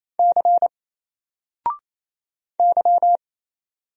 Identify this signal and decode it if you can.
Morse code, heard as a keyed tone, about 27 wpm, 700 Hz. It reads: C Y
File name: Single-Letter-Number-28wpm